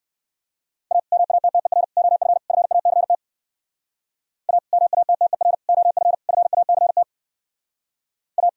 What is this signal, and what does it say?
Morse code, heard as a keyed tone, about 62 wpm, 700 Hz. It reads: A MATTER OF FACT A MATTER OF FACT A